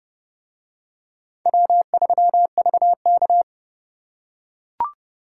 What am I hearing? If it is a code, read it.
Morse code, heard as a keyed tone, about 29 wpm, 700 Hz. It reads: W3VK